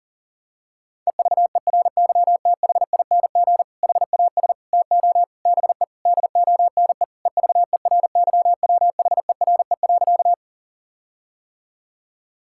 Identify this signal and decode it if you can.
Morse code, heard as a keyed tone, about 37 wpm, 700 Hz. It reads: EVERYTHING HAS TO BE DONE EVERYWHERE.
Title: Sentences from Top 1600 Words 40wpm